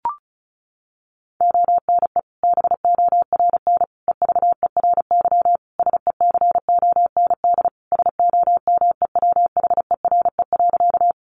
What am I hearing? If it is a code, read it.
Morse code, heard as a keyed tone, about 34 wpm, 700 Hz. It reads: ONE BORN EVERY SECOND SOMEWHERE.
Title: Sentences from Top 1400 Words 35wpm